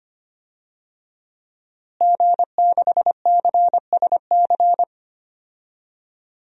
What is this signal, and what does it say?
Morse code, heard as a keyed tone, about 25 wpm, 700 Hz. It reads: G6CSC
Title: DXCC-100 Call Signs - 5 Characters - Repeated 3x - 25wpm